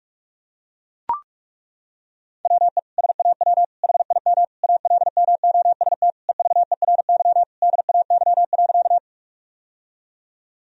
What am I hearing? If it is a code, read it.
Morse code, heard as a keyed tone, about 41 wpm, 700 Hz. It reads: WE SAW HIM ALMOST EVERY DAY.
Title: Sentences from Top 1000 Words - Review Part 4 - 45wpm